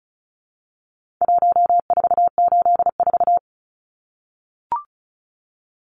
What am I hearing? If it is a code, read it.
Morse code, heard as a keyed tone, about 34 wpm, 700 Hz. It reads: 1484
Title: Numbers - Contest Style - 4-Digit 35wpm